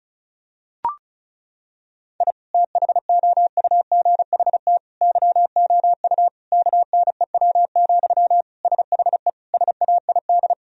Morse code, heard as a keyed tone, about 34 wpm, 700 Hz. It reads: I THOUGHT YOU KNEW, SHE SAID
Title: Sentences from Top 1000 Words - Review Part 4 - 35wpm